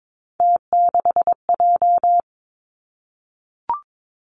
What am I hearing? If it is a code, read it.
Morse code, heard as a keyed tone, about 22 wpm, 700 Hz. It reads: T6J
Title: Three Challenging Letter-Number Combinations 22wpm